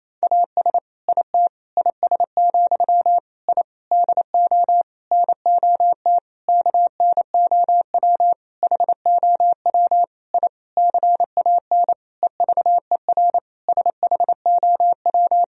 Morse code, heard as a keyed tone, about 28 wpm, 700 Hz. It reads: AS IT IS, I DO NOT KNOW HOW I CAN EVER SHOW
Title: Sentences from Top 1000 Words - Review Part 2 - 28wpm